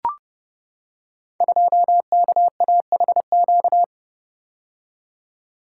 Morse code, heard as a keyed tone, about 29 wpm, 700 Hz. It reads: 2KAHQ
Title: Code Groups - 5 Letter:Numbers - 30wpm